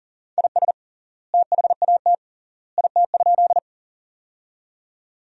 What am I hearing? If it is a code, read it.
Morse code, heard as a keyed tone, about 40 wpm, 700 Hz. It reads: IS THAT IT?
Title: 3x Word-Spacing - Sentences from Top 100 Words - 40wpm